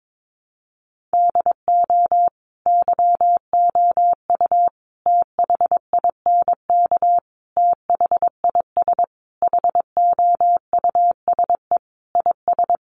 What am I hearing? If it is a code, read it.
Morse code, heard as a keyed tone, about 22 wpm, 700 Hz. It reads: DO YOU THINK THIS HOUSE IS
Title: Sentences from Top 1000 Words 22wpm